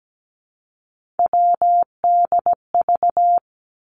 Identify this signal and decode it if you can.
Morse code, heard as a keyed tone, about 17 wpm, 700 Hz. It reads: WDV
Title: Three Challenging Letter-Number Combinations 17wpm